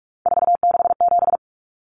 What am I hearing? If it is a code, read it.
Morse code, heard as a keyed tone, about 44 wpm, 700 Hz. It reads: 467
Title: Common Cut Numbers - 3-Digit 45wpm